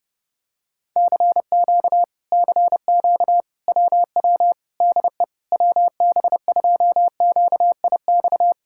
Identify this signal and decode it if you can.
Morse code, heard as a keyed tone, about 30 wpm, 700 Hz. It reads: CQ CQ WW DE WB2QIX